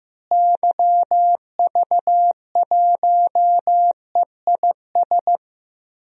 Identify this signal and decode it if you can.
Morse code, heard as a keyed tone, about 15 wpm, 700 Hz. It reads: YV1EIS
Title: DXCC-100 Call Signs - 6 Characters - Repeated 3x - 15wpm